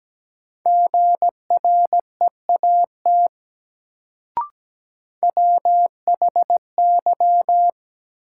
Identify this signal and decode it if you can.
Morse code, heard as a keyed tone, about 17 wpm, 700 Hz. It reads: GREAT WHY
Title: Sentences from Top 200 Words - New Word Drill - 17wpm